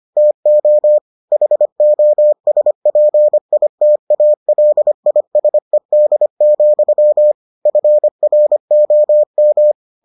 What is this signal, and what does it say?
Morse code, heard as a keyed tone, about 25 wpm, 600 Hz. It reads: TO HOSPITALISED, FROM